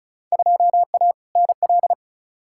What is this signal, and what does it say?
Morse code, heard as a keyed tone, about 36 wpm, 700 Hz. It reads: 2A NL
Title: Field Day Exchanges Encore - 35wpm